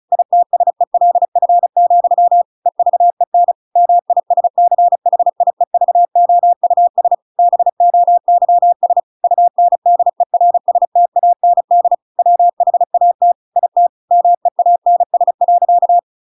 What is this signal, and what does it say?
Morse code, heard as a keyed tone, about 34 wpm, 700 Hz. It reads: ITSELF, EVEN MISCHIEVOUS BOYS UNDERSTAND WHAT IT MEANS.
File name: Pinocchio / Pinocchio ch8 35 35 ik1hge